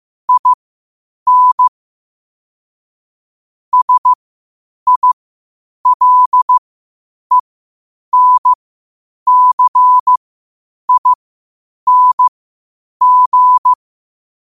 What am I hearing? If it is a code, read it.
Morse code, heard as a keyed tone, about 15 wpm, spaced as for 8 wpm, 1000 Hz. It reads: IN SILENCING
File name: qotdfiles / 8wpm